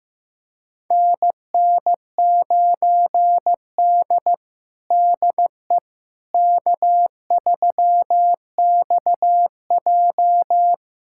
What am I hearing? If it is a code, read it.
Morse code, heard as a keyed tone, about 15 wpm, 700 Hz. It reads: NN9D DE K3XJ
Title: USA Callsign DE Callsign - 4 Characters - 15wpm